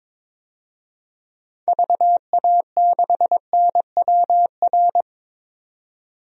Morse code, heard as a keyed tone, about 21 wpm, 700 Hz. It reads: VA6NWR
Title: DXCC-100 Call Signs - 6 Characters - 22wpm